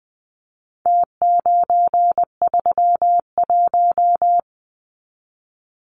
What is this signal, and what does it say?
Morse code, heard as a keyed tone, about 20 wpm, 700 Hz. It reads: T931
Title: Numbers - Contest Style - 4-Digit 20wpm